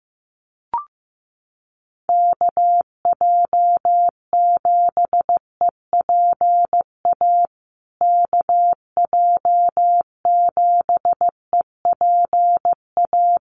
Morse code, heard as a keyed tone, about 15 wpm, 700 Hz. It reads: KJ7EPA KJ7EPA
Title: USA Call Signs Repeated 2x 15wpm